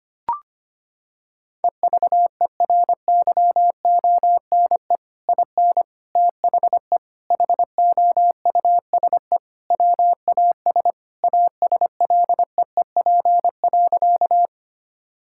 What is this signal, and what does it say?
Morse code, heard as a keyed tone, about 25 wpm, 700 Hz. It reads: EVERYONE IN THE HOUSE WAS ASLEEP.
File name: practice / 25wpm / Sentences with 3-Word Phrases - 25wpm